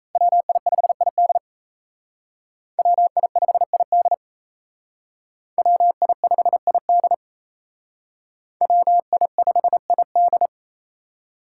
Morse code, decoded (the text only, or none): WI5ID WI5ID WI5ID WI5ID